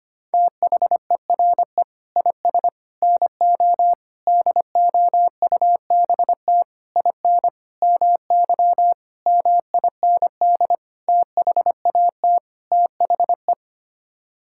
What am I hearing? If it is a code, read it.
Morse code, heard as a keyed tone, about 25 wpm, 700 Hz. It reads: THERE IS NO DOUBT IN MY MIND THAT THE
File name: Sets of 9 Words - 25wpm